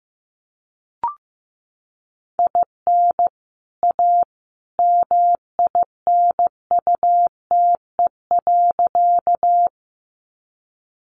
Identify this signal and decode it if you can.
Morse code, heard as a keyed tone, about 15 wpm, 700 Hz. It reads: IN A MINUTE.
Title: Sentences from Top 500 Words 15wpm